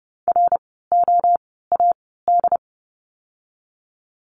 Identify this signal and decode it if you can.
Morse code, heard as a keyed tone, about 30 wpm, spaced as for 17 wpm, 700 Hz. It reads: ROAD
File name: Lesson 17f - Letter u - Words 30wpm